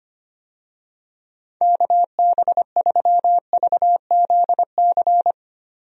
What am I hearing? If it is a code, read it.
Morse code, heard as a keyed tone, about 25 wpm, 700 Hz. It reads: KB3VZC
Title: USA Call Signs - 6 Characters - Repeated 3x - 25wpm